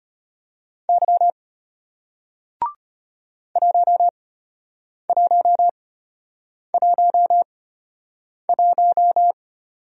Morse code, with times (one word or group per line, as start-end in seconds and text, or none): 0.9-1.3 Y
3.6-9.3 1111